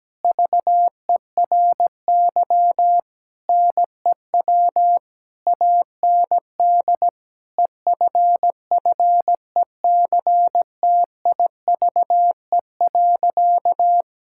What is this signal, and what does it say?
Morse code, heard as a keyed tone, about 17 wpm, 700 Hz. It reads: VERY NEW AND EFFECTIVE.
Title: Sentences from Top 1000 Words - Review Part 6 - 17wpm